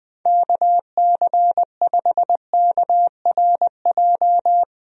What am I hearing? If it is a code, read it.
Morse code, heard as a keyed tone, about 20 wpm, 700 Hz. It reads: KC5KRJ
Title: USA Call Signs - 6 Characters - 20wpm